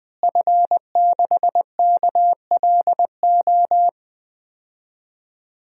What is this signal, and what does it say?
Morse code, heard as a keyed tone, about 21 wpm, 700 Hz. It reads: F6KLO
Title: DXCC-100 Call Signs - 5 Characters - Repeated 3x - 20wpm